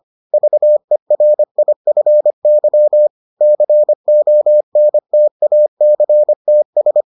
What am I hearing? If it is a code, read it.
Morse code, heard as a keyed tone, about 25 wpm, 600 Hz. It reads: VERIFY CONTACTS